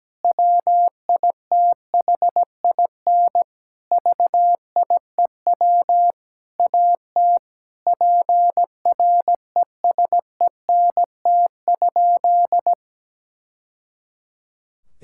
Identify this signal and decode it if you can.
Morse code, heard as a keyed tone, about 17 wpm, 700 Hz. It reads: WITHIN VIEW AT PRESENT?
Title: Sentences from Top 1000 Words - Review Part 6 - 17wpm